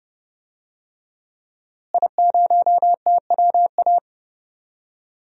Morse code, heard as a keyed tone, about 30 wpm, 700 Hz. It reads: I0TWA
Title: DXCC-100 Call Signs - 5 Characters - Repeated 3x - 30wpm